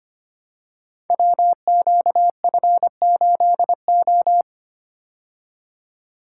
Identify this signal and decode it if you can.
Morse code, heard as a keyed tone, about 25 wpm, 700 Hz. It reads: WQF8O
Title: Code Groups - 5 Letter:Numbers - 25wpm